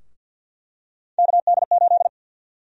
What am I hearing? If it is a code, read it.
Morse code, heard as a keyed tone, about 47 wpm, 700 Hz. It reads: KD8
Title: USA Call Sign Prefixes - 50wpm